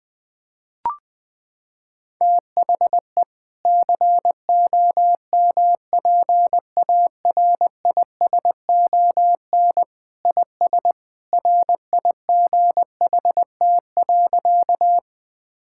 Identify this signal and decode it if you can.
Morse code, heard as a keyed tone, about 21 wpm, 700 Hz. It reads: THE COMPARISON IS RIGHT.
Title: Sentences from Top 1600 Words 20wpm